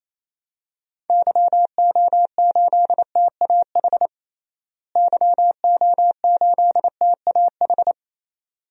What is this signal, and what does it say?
Morse code, heard as a keyed tone, about 28 wpm, 700 Hz. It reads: YO8TAH YO8TAH
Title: DXCC-100 Call Signs - 6 Characters - Repeated 2x - 28wpm